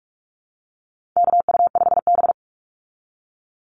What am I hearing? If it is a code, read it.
Morse code, heard as a keyed tone, about 44 wpm, 700 Hz. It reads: KU5B
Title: Call Signs Repeated 2x - 45wpm